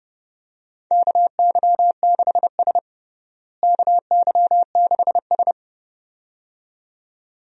Morse code, heard as a keyed tone, about 31 wpm, 700 Hz. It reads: KY6S KY6S
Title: USA Call Signs - 4 Characters - Repeated 2x - 30wpm